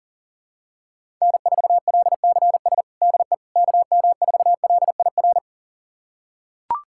Morse code, heard as a keyed tone, about 40 wpm, 700 Hz. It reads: N4LCS DE KM4LIR
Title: USA Callsign DE Callsign 40wpm